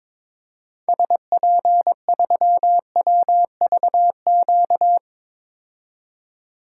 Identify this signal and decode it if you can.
Morse code, heard as a keyed tone, about 22 wpm, 700 Hz. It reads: SP3WVQ
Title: DXCC-100 Call Signs - 6 Characters - 22wpm